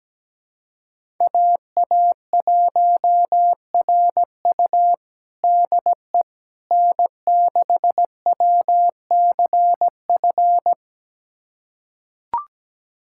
Morse code, heard as a keyed tone, about 17 wpm, 700 Hz. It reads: AA1RU DE N6WCF
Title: USA Callsign DE Callsign - 5 Characters - 17wpm